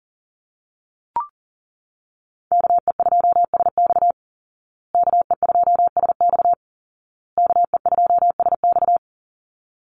Morse code, heard as a keyed tone, about 40 wpm, 700 Hz. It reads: KE2SX KE2SX KE2SX
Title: Call Signs Repeated 2x - 40wpm